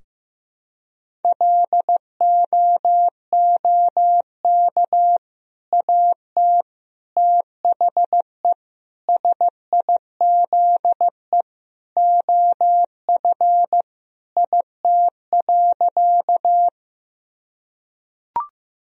Morse code, heard as a keyed tone, about 15 wpm, 700 Hz. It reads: LOOK AT THE SIZE OF IT.
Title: Sentences from Top 1000 Words - Review Part 2 - 15wpm